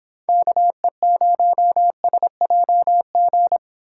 Morse code, heard as a keyed tone, about 26 wpm, 700 Hz. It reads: KE0SJG